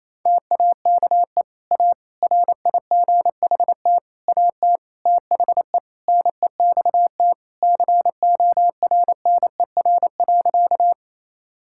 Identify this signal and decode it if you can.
Morse code, heard as a keyed tone, about 28 wpm, 700 Hz. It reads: TAKE A RIGHT AT THE NEXT CORNER.
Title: Sentences from Top 1600 Words 28wpm